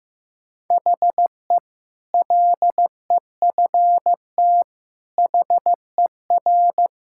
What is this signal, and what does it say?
Morse code, heard as a keyed tone, about 15 wpm, 700 Hz. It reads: HE LEFT HER
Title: Sentences from Top 1000 Words - Review Part 6 - 15wpm